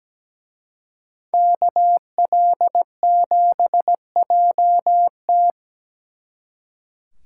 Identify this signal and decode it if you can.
Morse code, heard as a keyed tone, about 21 wpm, 700 Hz. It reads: KL7JT